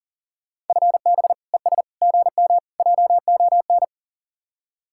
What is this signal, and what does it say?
Morse code, heard as a keyed tone, about 49 wpm, 700 Hz. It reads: FB ES GM JON
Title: Speed-Racing Common QSO Phrases Encore - 40wpm